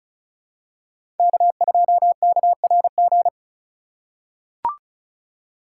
Morse code, heard as a keyed tone, about 34 wpm, 700 Hz. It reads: K2KRG
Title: USA Call Signs - 5 Characters - 3x Character Spacing - 35wpm